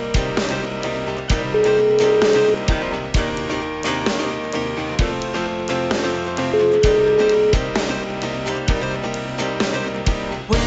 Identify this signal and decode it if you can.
Ringback tone (Europe, 425 Hz), with noise.